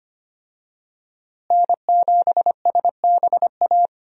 Morse code, heard as a keyed tone, about 25 wpm, 700 Hz. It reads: N7SBA
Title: USA Call Signs - 5 Characters - Repeated 3x - 25wpm